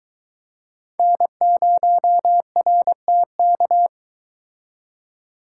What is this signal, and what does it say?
Morse code, heard as a keyed tone, about 22 wpm, 700 Hz. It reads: N0RTK